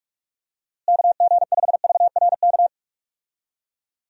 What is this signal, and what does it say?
Morse code, heard as a keyed tone, about 43 wpm, 700 Hz. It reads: KG5VRK
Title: USA Call Signs - 6 Characters - Repeated 2x - 45wpm